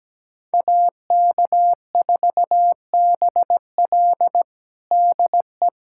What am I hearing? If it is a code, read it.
Morse code, heard as a keyed tone, about 17 wpm, 700 Hz. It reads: AK4BL DE
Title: USA Callsign DE Callsign 17wpm